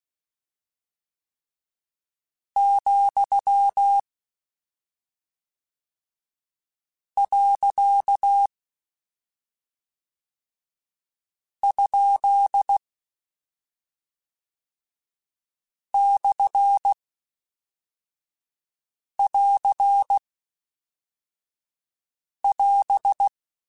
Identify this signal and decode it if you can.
Morse code, heard as a keyed tone, about 16 wpm, 800 Hz. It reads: , . ? / <AR> <AS>
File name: Short Runs / Short Runs-mp3s / Words @10